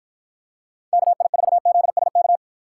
Morse code, heard as a keyed tone, about 53 wpm, 700 Hz. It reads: KI4ZSK